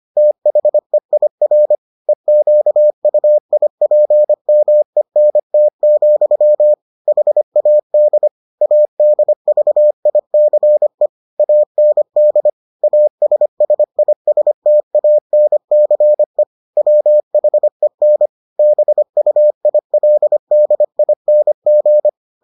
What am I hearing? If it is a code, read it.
Morse code, heard as a keyed tone, about 25 wpm, 600 Hz. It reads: THEIR EQUIPMENT, HAD ADVICE AND ASSISTANCE WHEN BUILDING